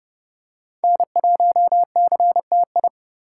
Morse code, heard as a keyed tone, about 28 wpm, 700 Hz. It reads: N1CTI